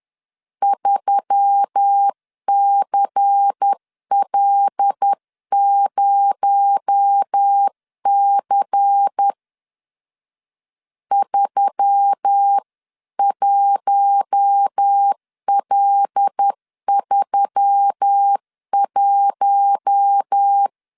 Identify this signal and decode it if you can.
Morse code, heard as a keyed tone, about 10 wpm, 800 Hz. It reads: 3CL0C 31L31
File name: CW / DET / 1 6 1500C S C 10wpm 4252